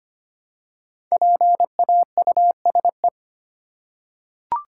Morse code, heard as a keyed tone, about 25 wpm, 700 Hz. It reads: PAUSE
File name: 5 Letter Words - First 3 Letters Given - 25wpm